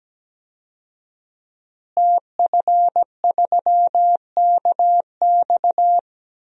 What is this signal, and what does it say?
Morse code, heard as a keyed tone, about 17 wpm, 700 Hz. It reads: TF3KX
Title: DXCC-100 Call Signs - 5 Characters - Repeated 2x - 17wpm